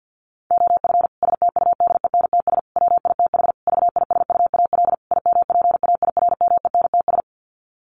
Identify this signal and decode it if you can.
Morse code, heard as a keyed tone, about 50 wpm, 700 Hz. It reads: OF STUDENTS WITH VISUAL IMPAIRMENTS
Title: Sets of 5 Words Encore 50wpm